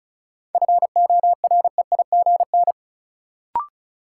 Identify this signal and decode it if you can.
Morse code, heard as a keyed tone, about 36 wpm, 700 Hz. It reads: FOREIGN
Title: Sentences from Top 1400 Words - New Word Drill - 35wpm